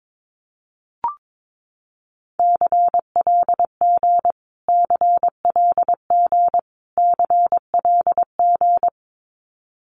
Morse code, heard as a keyed tone, about 22 wpm, 700 Hz. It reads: CLG CLG CLG
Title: Abbreviations Repeated 3x - 22wpm